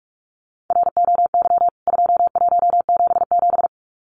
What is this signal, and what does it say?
Morse code, heard as a keyed tone, about 44 wpm, 700 Hz. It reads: ROY 2177
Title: CWT Exchange 45wpm